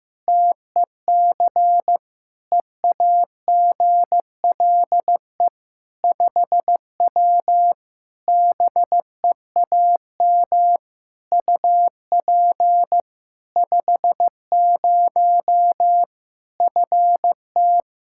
Morse code, heard as a keyed tone, about 15 wpm, 700 Hz. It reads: TEC EAGLE 5W BEAM UP 50 FT
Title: Practice QSO - Exchange 2 - Info - 15wpm